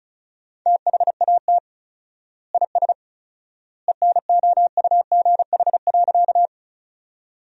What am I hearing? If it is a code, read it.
Morse code, heard as a keyed tone, about 36 wpm, 700 Hz. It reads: THAT IS ENOUGH.
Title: 4x Word-Spacing - Sentences from Top 400 Words - 35wpm